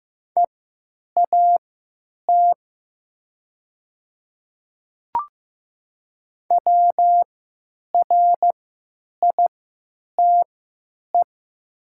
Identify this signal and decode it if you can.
Morse code, heard as a keyed tone, about 15 wpm, spaced as for 9 wpm, 700 Hz. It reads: EAT WRITE
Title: 3x Character Spacing - Top 300 Words - 15wpm